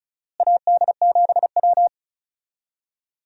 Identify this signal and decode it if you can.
Morse code, heard as a keyed tone, about 34 wpm, 700 Hz. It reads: AD7W
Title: USA Call Signs - 4 Characters - Repeated 2x - 35wpm